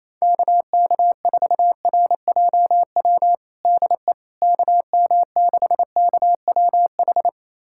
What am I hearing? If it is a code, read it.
Morse code, heard as a keyed tone, about 28 wpm, 700 Hz. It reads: KK4RJW DE KM6KWH
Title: USA Callsign DE Callsign 28wpm